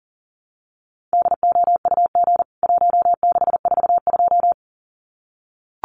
Morse code, heard as a keyed tone, about 40 wpm, 700 Hz. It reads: DOUG 1642
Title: CWT Exchange 40wpm